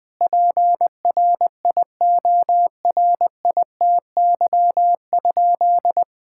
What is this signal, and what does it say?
Morse code, heard as a keyed tone, about 20 wpm, 700 Hz. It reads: PRIORITY?
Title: Sentences from Top 1600 Words 20wpm